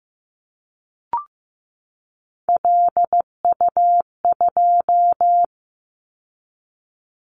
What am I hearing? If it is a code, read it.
Morse code, heard as a keyed tone, about 15 wpm, 700 Hz. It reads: LU2
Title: Callsign Prefixes for Top 50 DXCC - 15wpm